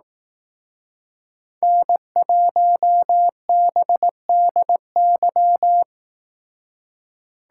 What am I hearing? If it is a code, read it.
Morse code, heard as a keyed tone, about 18 wpm, 700 Hz. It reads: N1BDY
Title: Speed Racing - USA Call Signs 15wpm